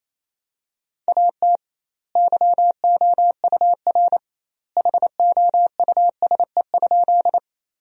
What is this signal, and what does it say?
Morse code, heard as a keyed tone, about 28 wpm, 700 Hz. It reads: AT YOUR HOUSE?
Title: 2x Word-Spacing - Sentences from Top 200 Words - 28wpm